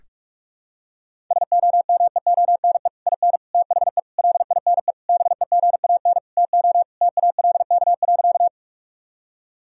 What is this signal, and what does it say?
Morse code, heard as a keyed tone, about 46 wpm, 700 Hz. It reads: SOMEONE IN THE LINE BEGAN TO TALK.
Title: Sentences from Top 1000 Words - Review Part 1 - 45wpm